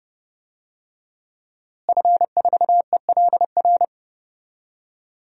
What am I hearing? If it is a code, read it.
Morse code, heard as a keyed tone, about 32 wpm, 700 Hz. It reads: F4ELR
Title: DXCC-100 Call Signs - 5 Characters - Repeated 3x - 30wpm